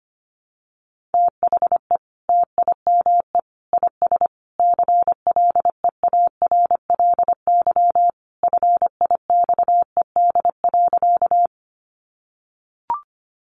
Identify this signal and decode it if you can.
Morse code, heard as a keyed tone, about 25 wpm, 700 Hz. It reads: THE TIME IS CLEARLY FIXED.
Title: Sentences from Top 1300 Words 25wpm